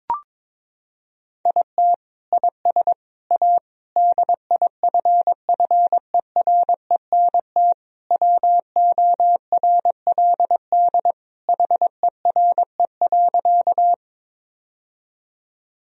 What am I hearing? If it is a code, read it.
Morse code, heard as a keyed tone, about 22 wpm, 700 Hz. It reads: IT IS A DIFFERENT WORLD HERE.
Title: Sentences from Top 1000 Words - Review Part 3 - 22wpm